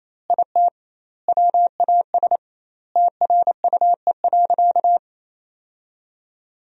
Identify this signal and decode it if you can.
Morse code, heard as a keyed tone, about 28 wpm, 700 Hz. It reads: IT WAS TRUE.
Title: 2x Word-Spacing - Sentences from Top 400 Words - 28wpm